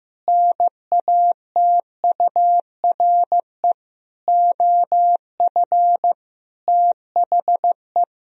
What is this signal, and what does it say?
Morse code, heard as a keyed tone, about 15 wpm, 700 Hz. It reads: NATURE OF THE
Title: Sentences from Top 1000 Words - Review Part 3 - 15wpm